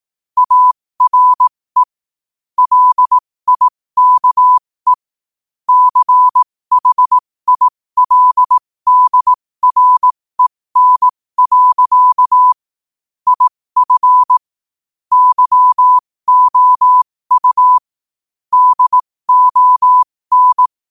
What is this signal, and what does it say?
Morse code, heard as a keyed tone, about 18 wpm, spaced as for 15 wpm, 1000 Hz. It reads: ARE LIKE CHILDREN. IF YOU DON
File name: qotdfiles / 15wpm